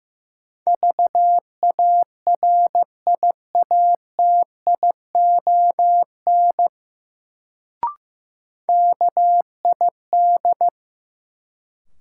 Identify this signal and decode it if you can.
Morse code, heard as a keyed tone, about 15 wpm, 700 Hz. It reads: VARIATION KID
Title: Sentences from Top 1600 Words - New Word Drill - 15wpm